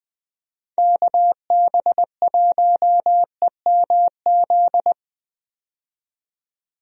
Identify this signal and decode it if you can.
Morse code, heard as a keyed tone, about 20 wpm, 700 Hz. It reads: KB1EMZ
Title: USA Call Signs - 6 Characters - 20wpm